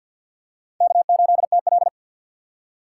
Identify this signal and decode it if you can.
Morse code, heard as a keyed tone, about 53 wpm, 700 Hz. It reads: K8TL
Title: USA Call Signs - 4 Characters - 6x Character Spacing - 50wpm